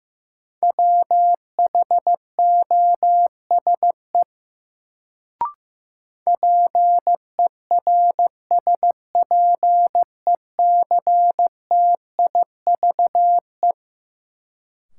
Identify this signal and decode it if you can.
Morse code, heard as a keyed tone, about 15 wpm, 700 Hz. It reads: WHOSE PERSPECTIVE
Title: Sentences from Top 1200 Words - New Word Drill - 15wpm